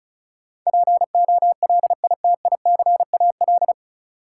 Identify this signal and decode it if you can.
Morse code, heard as a keyed tone, about 34 wpm, 700 Hz. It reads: POLITICAL
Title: Sentences from Top 800 Words - New Word Drill - 35wpm